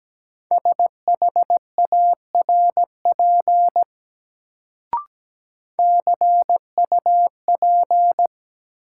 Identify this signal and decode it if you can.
Morse code, heard as a keyed tone, about 17 wpm, 700 Hz. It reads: SHARP CUP
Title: Sentences from Top 1600 Words - New Word Drill Alt - 17wpm